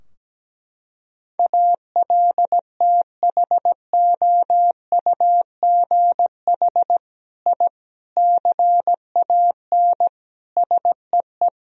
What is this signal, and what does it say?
Morse code, heard as a keyed tone, about 17 wpm, 700 Hz. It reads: ALTHOUGH I CAN SEE
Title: Sentences from Top 1000 Words - Review Part 3 - 17wpm